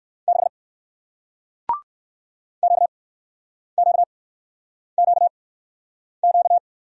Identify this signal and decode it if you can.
Morse code, heard as a keyed tone, about 58 wpm, spaced as for 9 wpm, 700 Hz. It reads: 6 QQQQ